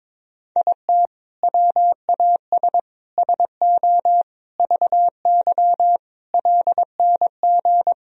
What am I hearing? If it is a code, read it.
Morse code, heard as a keyed tone, about 22 wpm, 700 Hz. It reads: IT WAS SO VY LNG
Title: Sentences from Top 500 Words and Abbreviations 22wpm